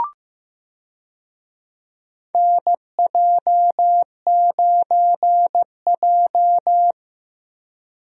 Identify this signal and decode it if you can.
Morse code, heard as a keyed tone, about 15 wpm, 700 Hz. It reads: NJ9J
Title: USA Call Signs - 4 Characters - Repeated 2x - 15wpm